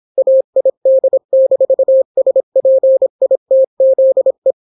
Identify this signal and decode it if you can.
Morse code, heard as a keyed tone, about 26 wpm, 530 Hz. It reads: AID-SPITZE